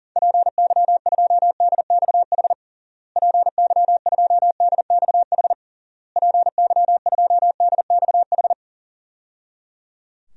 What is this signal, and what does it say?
Morse code, heard as a keyed tone, about 40 wpm, 700 Hz. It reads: PY2DXH PY2DXH PY2DXH
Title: DXCC-100 Call Signs - 6 Characters - Repeated 3x - 40wpm